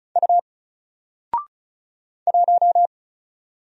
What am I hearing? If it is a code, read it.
Morse code, heard as a keyed tone, about 36 wpm, 700 Hz. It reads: U 1
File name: Single-Letter-Number-35wpm